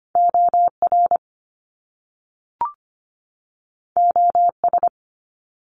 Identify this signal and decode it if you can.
Morse code, heard as a keyed tone, about 25 wpm, 700 Hz. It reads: OR OS
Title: Top Two Letter Combinations in Words 25wpm